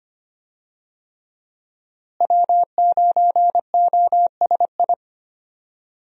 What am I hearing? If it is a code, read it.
Morse code, heard as a keyed tone, about 25 wpm, 700 Hz. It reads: W9OSI
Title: USA Call Signs - 5 Characters - Repeated 3x - 25wpm